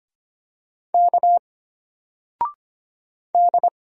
Morse code, heard as a keyed tone, about 25 wpm, 700 Hz. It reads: K D
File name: Single-Letter-Number-25wpm